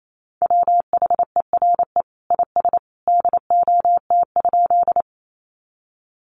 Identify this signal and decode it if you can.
Morse code, heard as a keyed tone, about 28 wpm, 700 Hz. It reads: WHERE IS DOT?
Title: Sentences from Top 1500 Words 28wpm